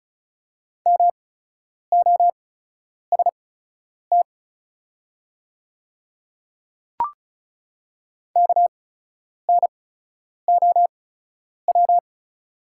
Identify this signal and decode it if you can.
Morse code, heard as a keyed tone, about 36 wpm, spaced as for 10 wpm, 690 Hz. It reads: MOST KNOW
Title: 8x Character Spacing - Top 100 Words - 35wpm